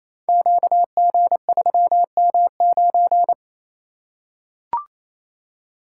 Morse code, heard as a keyed tone, about 27 wpm, 700 Hz. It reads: QG3M9
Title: Code Groups - 5 Letter:Numbers - 28wpm